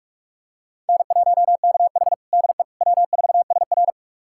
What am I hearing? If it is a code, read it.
Morse code, heard as a keyed tone, about 42 wpm, 700 Hz. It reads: N1KH DE W4SR
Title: USA Callsign DE Callsign - 4 Characters - 45wpm